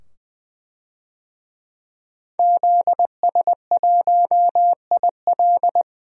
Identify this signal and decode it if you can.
Morse code, heard as a keyed tone, about 20 wpm, 700 Hz. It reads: ZS1IL
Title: DXCC-100 Call Signs - 5 Characters - 20wpm